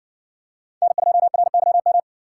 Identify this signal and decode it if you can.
Morse code, heard as a keyed tone, about 60 wpm, 700 Hz. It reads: N2RYM